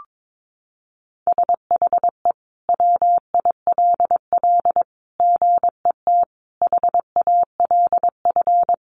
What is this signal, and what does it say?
Morse code, heard as a keyed tone, about 22 wpm, 700 Hz. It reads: SHE WILL GET HALF
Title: Sentences from Top 1200 Words 22wpm